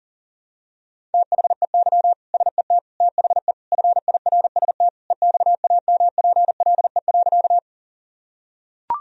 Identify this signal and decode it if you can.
Morse code, heard as a keyed tone, about 40 wpm, 700 Hz. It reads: THEY SET THE FIRST EXAMPLE.
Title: Sentences from Top 1000 Words - Review Part 1 - 40wpm